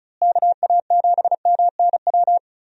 Morse code, heard as a keyed tone, about 34 wpm, 700 Hz. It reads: KA7MNW
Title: USA Call Signs - 6 Characters - Repeated 3x - 35wpm